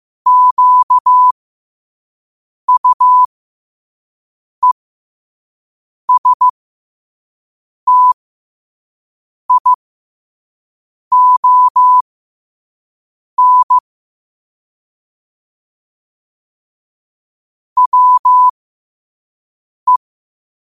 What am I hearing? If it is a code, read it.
Morse code, heard as a keyed tone, about 15 wpm, spaced as for 5 wpm, 1000 Hz. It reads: QUESTION WE
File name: qotdfiles / 5wpm